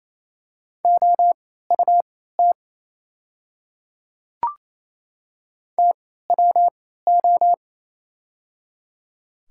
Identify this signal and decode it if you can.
Morse code, heard as a keyed tone, about 28 wpm, spaced as for 16 wpm, 690 Hz. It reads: OUT TWO
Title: 3x Character Spacing - Top 100 Words - 28wpm